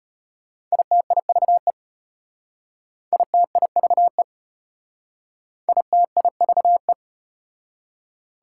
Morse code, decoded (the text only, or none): ITIVE ITIVE ITIVE